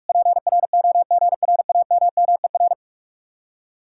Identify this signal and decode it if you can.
Morse code, heard as a keyed tone, about 58 wpm, 700 Hz. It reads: PROGRAMMER